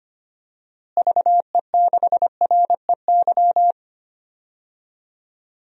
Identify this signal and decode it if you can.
Morse code, heard as a keyed tone, about 25 wpm, 700 Hz. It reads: VE6REY
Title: DXCC-100 Call Signs - 6 Characters - Repeated 3x - 25wpm